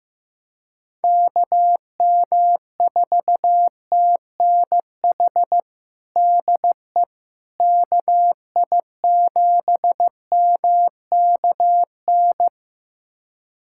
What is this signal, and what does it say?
Morse code, heard as a keyed tone, about 15 wpm, 700 Hz. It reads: KM4TNH DE KI7MKN
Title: USA Callsign DE Callsign - 6 Characters - 15wpm